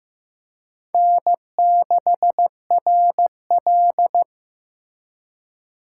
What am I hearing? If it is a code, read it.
Morse code, heard as a keyed tone, about 18 wpm, 700 Hz. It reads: N6RL
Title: Speed Racing - USA Call Signs 15wpm